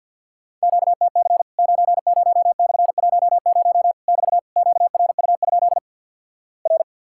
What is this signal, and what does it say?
Morse code, heard as a keyed tone, about 50 wpm, 700 Hz. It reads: QTC 90/10 <BT> QRU? R
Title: Worked All Europe DX Contest - QTC Exchange with Repeats - 50wpm